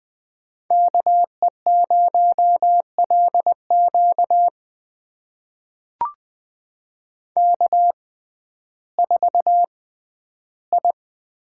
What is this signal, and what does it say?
Morse code, heard as a keyed tone, about 20 wpm, spaced as for 12 wpm, 700 Hz. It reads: KE0LQ K4I
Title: USA Call Signs - 5 Characters - 6x Character Spacing - 20wpm